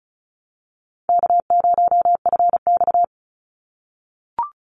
Morse code, heard as a keyed tone, about 34 wpm, 700 Hz. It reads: K0FX
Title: Call Signs - 35wpm